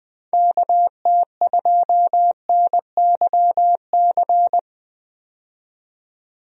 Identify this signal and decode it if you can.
Morse code, heard as a keyed tone, about 20 wpm, 700 Hz. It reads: KT2NYC
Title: USA Call Signs - 6 Characters - 20wpm